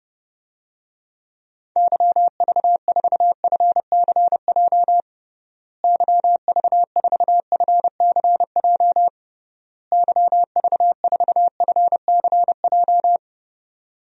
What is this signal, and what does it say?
Morse code, heard as a keyed tone, about 32 wpm, 700 Hz. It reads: YV4FCJ YV4FCJ YV4FCJ
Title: DXCC-100 Call Signs - 6 Characters - Repeated 3x - 30wpm